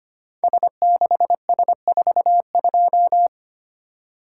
Morse code, heard as a keyed tone, about 25 wpm, 700 Hz. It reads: S6S42
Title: Code Groups - 5 Letter:Numbers - 25wpm